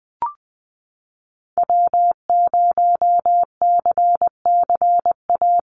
Morse code, heard as a keyed tone, about 20 wpm, 700 Hz. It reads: W0CCA
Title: NA SOTA Activator Callsigns 20wpm